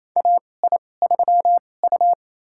Morse code, heard as a keyed tone, about 28 wpm, spaced as for 20 wpm, 700 Hz. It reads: AI3U
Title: USA Call Signs - 4 Characters - 2x Character Spacing - 28wpm